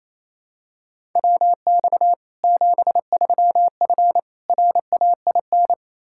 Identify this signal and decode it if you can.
Morse code, heard as a keyed tone, about 28 wpm, 700 Hz. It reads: WX 73F RAIN
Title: Practice QSO - Exchange 2 - Info - 28wpm